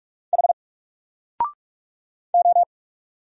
Morse code, heard as a keyed tone, about 44 wpm, 700 Hz. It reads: H O
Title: Single Character (Letter:Number:Punctuation:Prosign) Alt 45wpm